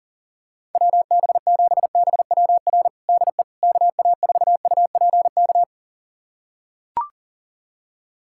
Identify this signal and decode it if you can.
Morse code, heard as a keyed tone, about 37 wpm, 700 Hz. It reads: WB7BWR DE KA4UPK
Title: USA Callsign DE Callsign 40wpm